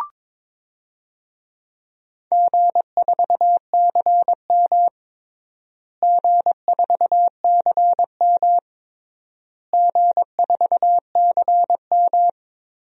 Morse code, heard as a keyed tone, about 22 wpm, 700 Hz. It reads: G4CM G4CM G4CM
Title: DXCC-100 Call Signs - 4 Characters - Repeated 3x - 22wpm